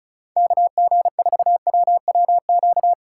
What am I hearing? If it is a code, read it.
Morse code, heard as a keyed tone, about 36 wpm, 700 Hz. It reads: KG4WWQ
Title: USA Call Signs - 6 Characters - 4x Character Spacing - 35wpm